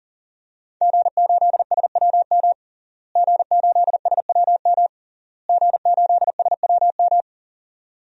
Morse code, heard as a keyed tone, about 37 wpm, 700 Hz. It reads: G8SWM G8SWM G8SWM
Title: DXCC-100 Call Signs - 5 Characters - Repeated 3x - 40wpm